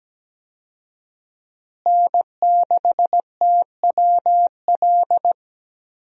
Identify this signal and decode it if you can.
Morse code, heard as a keyed tone, about 17 wpm, 700 Hz. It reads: N6TWL
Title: USA Call Signs - 5 Characters - Repeated 3x - 17wpm